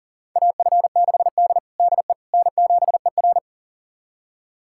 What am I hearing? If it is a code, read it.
Morse code, heard as a keyed tone, about 40 wpm, 700 Hz. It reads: AF6D DE N7ER
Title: USA Callsign DE Callsign - 4 Characters - 40wpm